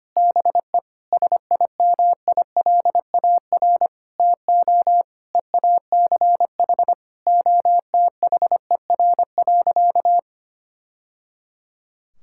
Morse code, heard as a keyed tone, about 25 wpm, 700 Hz. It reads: BE SIMILAR TO EACH OTHER.
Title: Sentences from Top 1000 Words - Review Part 5 - 25wpm